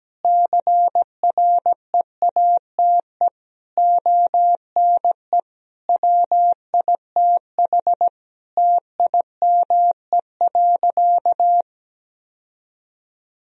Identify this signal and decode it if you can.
Morse code, heard as a keyed tone, about 17 wpm, 700 Hz. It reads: CREATE ONE WITH TIME.
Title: Sentences from Top 1000 Words - Review Part 3 - 17wpm